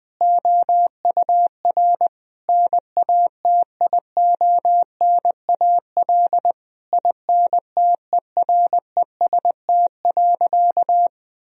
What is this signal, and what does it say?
Morse code, heard as a keyed tone, about 20 wpm, 700 Hz. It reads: OUR NATIONAL INTEREST.
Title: Sentences from Top 1000 Words - Review Part 4 - 20wpm